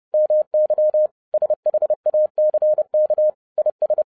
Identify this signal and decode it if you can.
Morse code, heard as a keyed tone, about 30 wpm, 610 Hz. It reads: MY SHACK IS
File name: 2011-07-mp3 / n20 GREETINGS FROM SINGAPORE 30wpm